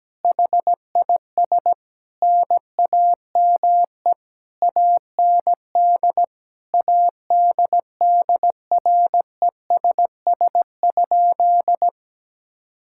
Morse code, heard as a keyed tone, about 17 wpm, 700 Hz. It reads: HIS NAME AND ADDRESS?
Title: Sentences from Top 1000 Words - Review Part 6 - 17wpm